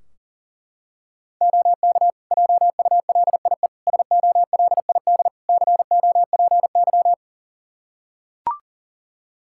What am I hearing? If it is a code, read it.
Morse code, heard as a keyed tone, about 40 wpm, 700 Hz. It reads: OK JULIE SOLID COPY
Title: Practice QSO - Exchange 3 - More Info - 40wpm